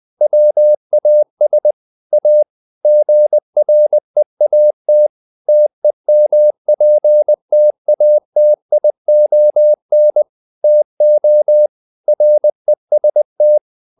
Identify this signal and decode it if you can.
Morse code, heard as a keyed tone, about 20 wpm, 600 Hz. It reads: WAS A GREAT TEMPTATION TO REST
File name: the old man and the sea part 2 20wpm